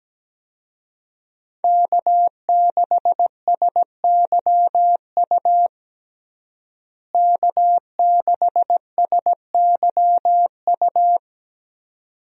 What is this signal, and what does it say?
Morse code, heard as a keyed tone, about 17 wpm, 700 Hz. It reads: K6SYU K6SYU
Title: USA Call Signs - 5 Characters - Repeated 2x - 17wpm